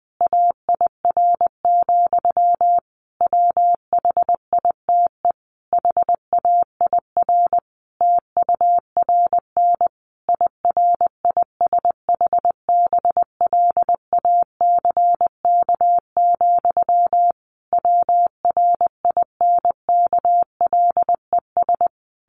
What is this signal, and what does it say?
Morse code, heard as a keyed tone, about 20 wpm, 700 Hz. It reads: AIR, WHITE HAIR TURN IRISH BLACK, WRINKLES
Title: A Sound Of Thunder 20wpm